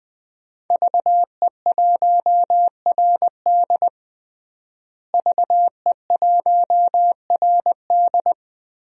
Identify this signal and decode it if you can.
Morse code, heard as a keyed tone, about 20 wpm, 700 Hz. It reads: VE1RD VE1RD
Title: DXCC-100 Call Signs - 5 Characters - Repeated 3x - 20wpm